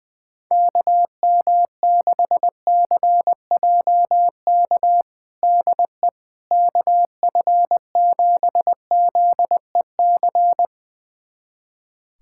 Morse code, heard as a keyed tone, about 20 wpm, 700 Hz. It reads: KM6CJK DE KF7ZEC
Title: USA Callsign DE Callsign - 6 Characters - 20wpm